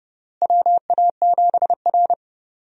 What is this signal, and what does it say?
Morse code, heard as a keyed tone, about 30 wpm, 700 Hz. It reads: WA7R
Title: USA Call Signs - 4 Characters - Repeated 2x - 30wpm